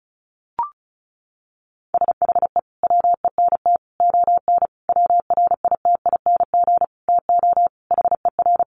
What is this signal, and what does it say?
Morse code, heard as a keyed tone, about 34 wpm, 700 Hz. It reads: SHE WENT ON WRITING TO HER
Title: Sentences from Top 800 Words 35wpm